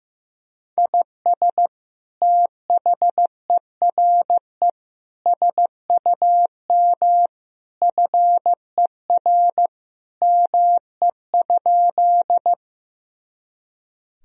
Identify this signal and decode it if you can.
Morse code, heard as a keyed tone, about 15 wpm, 700 Hz. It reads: IS THERE SUM FER ME?
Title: Sentences from Top 200 Words and Abbreviations 15wpm